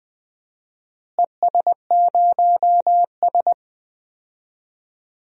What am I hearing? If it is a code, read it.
Morse code, heard as a keyed tone, about 19 wpm, 700 Hz. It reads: ES0S
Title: DXCC-100 Call Signs - 4 Characters - Repeated 3x - 20wpm